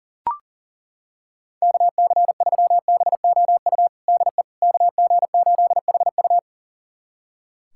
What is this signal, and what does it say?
Morse code, heard as a keyed tone, about 40 wpm, 700 Hz. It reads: KC3BOU DE KG8HU
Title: USA Callsign DE Callsign 40wpm